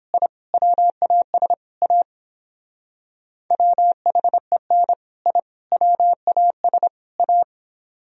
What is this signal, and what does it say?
Morse code, heard as a keyed tone, about 29 wpm, 700 Hz. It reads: I WAS A WHEN I WAS A